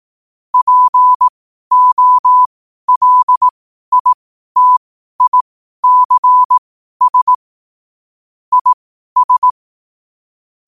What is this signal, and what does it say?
Morse code, heard as a keyed tone, about 18 wpm, spaced as for 12 wpm, 1000 Hz. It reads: POLITICS IS
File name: qotdfiles / 12wpm